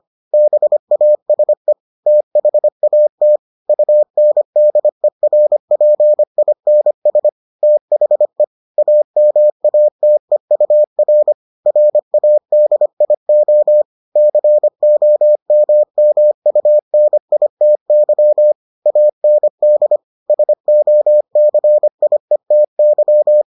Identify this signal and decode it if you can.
Morse code, heard as a keyed tone, about 25 wpm, 600 Hz. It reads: BASE THAT UNDERPINS THE AMATEUR RADIO COMMUNITY AND SOCIETY